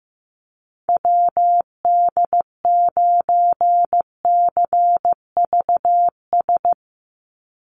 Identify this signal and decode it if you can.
Morse code, heard as a keyed tone, about 15 wpm, 700 Hz. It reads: WD9CVS
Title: USA Call Signs 15wpm